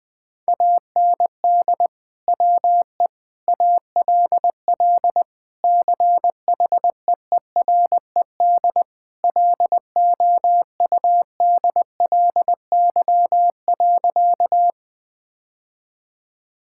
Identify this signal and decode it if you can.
Morse code, heard as a keyed tone, about 20 wpm, 700 Hz. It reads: AND WE ALL CHEERED LOUDLY.
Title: Sentences with 2-Word Phrases - 20wpm